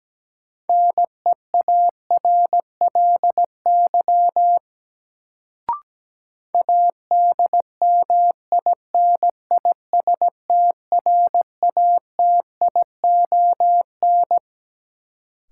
Morse code, heard as a keyed tone, about 17 wpm, 700 Hz. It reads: NEARLY ADMINISTRATION
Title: Sentences from Top 1300 Words - New Word Drill Alt - 17wpm